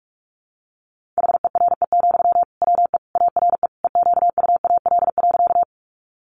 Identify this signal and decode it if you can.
Morse code, heard as a keyed tone, about 44 wpm, 700 Hz. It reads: HERE, WE ARE EQUAL.
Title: Sentences from Top 1400 Words 45wpm